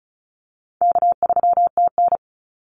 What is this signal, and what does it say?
Morse code, heard as a keyed tone, about 34 wpm, 700 Hz. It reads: K3TN
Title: Call Signs - 35wpm